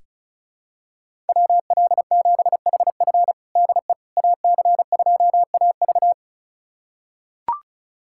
Morse code, heard as a keyed tone, about 34 wpm, 700 Hz. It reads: WL7HF DE AC2AV
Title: USA Callsign DE Callsign - 5 Characters - 35wpm